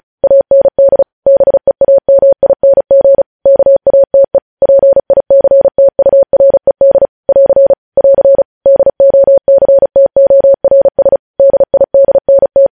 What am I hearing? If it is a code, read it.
Morse code, heard as a keyed tone, about 36 wpm, 550 Hz. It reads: AND BEAMING KATE PICTURED <AR> <AR> DOCTORS DIDNT
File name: podcast / News headlines-1x-35x35